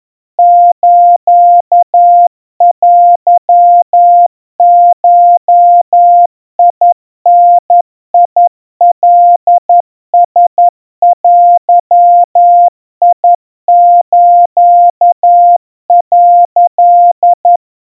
Morse code (Japanese Wabun code), heard as a keyed tone, about 11 wpm, 700 Hz. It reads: ステゴダカラデス」
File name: kawa / Morse sample / kamyu 40 cpm 700 Hz Crispy